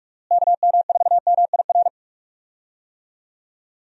Morse code, heard as a keyed tone, about 47 wpm, 700 Hz. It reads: KM4MIR
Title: USA Call Signs - 6 Characters - Repeated 2x - 45wpm